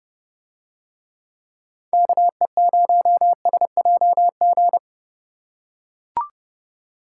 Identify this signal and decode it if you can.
Morse code, heard as a keyed tone, about 29 wpm, 700 Hz. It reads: KE0SJG